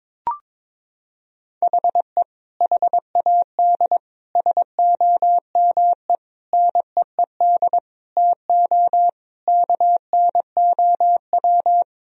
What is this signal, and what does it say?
Morse code, heard as a keyed tone, about 22 wpm, 700 Hz. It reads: HE HAD SOME NEED TO KNOW
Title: Sentences from Top 1000 Words - Review Part 3 - 22wpm